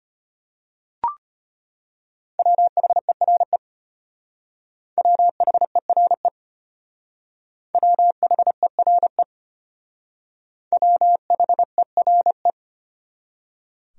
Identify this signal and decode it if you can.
Morse code, heard as a keyed tone, about 31 wpm, 700 Hz. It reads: WHERE WHERE WHERE WHERE